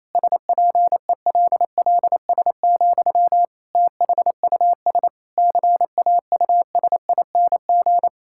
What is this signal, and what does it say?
Morse code, heard as a keyed tone, about 28 wpm, 700 Hz. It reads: SPELLS, THUS CAUSING
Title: Pangram Challenge v2 - 28wpm